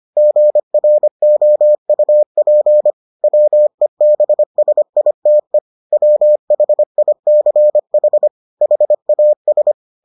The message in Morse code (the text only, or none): GROUP WEBSITE WHICH HAS